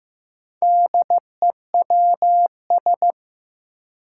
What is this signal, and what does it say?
Morse code, heard as a keyed tone, about 15 wpm, 700 Hz. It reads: DEWS
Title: Four Challenging Letter-Number Combinations 15wpm